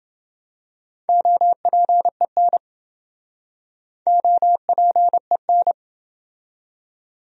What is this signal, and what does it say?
Morse code, heard as a keyed tone, about 28 wpm, 700 Hz. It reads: OPEN OPEN